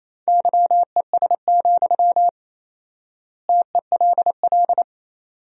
Morse code, heard as a keyed tone, about 28 wpm, 700 Hz. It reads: YES, TELL
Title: 4x Word-Spacing - Sentences from Top 500 Words - 28wpm